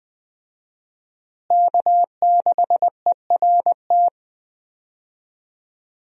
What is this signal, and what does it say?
Morse code, heard as a keyed tone, about 20 wpm, 700 Hz. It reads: K6ERT
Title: USA Call Signs - 5 Characters - Repeated 3x - 20wpm